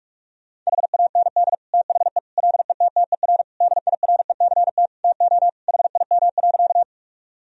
Morse code, heard as a keyed tone, about 46 wpm, 700 Hz. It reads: HAND THE LETTER DIRECT TO HIM.
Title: Sentences from Top 1000 Words - Review Part 2 - 45wpm